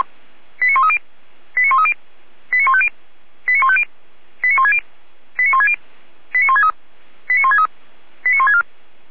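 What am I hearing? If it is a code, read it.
Five-tone selective call (ZVEI), heard as quick runs of five tones, tones of about 70 ms. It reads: 89130 89130 89150 89150 89160 89160 89163 89163 89164
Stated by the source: ZVEI five-tone call 89120 received by SDR.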